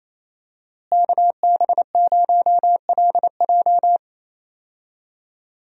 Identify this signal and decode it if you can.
Morse code, heard as a keyed tone, about 28 wpm, 700 Hz. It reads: KB0LJ
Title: USA Call Signs - 5 Characters - Repeated 2x - 28wpm